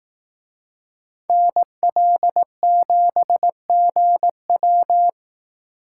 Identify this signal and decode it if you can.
Morse code, heard as a keyed tone, about 18 wpm, 700 Hz. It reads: NL7GW
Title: Speed Racing - USA 2x2 Call Signs 15wpm